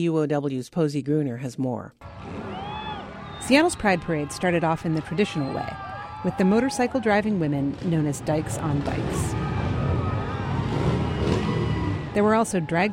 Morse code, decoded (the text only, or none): none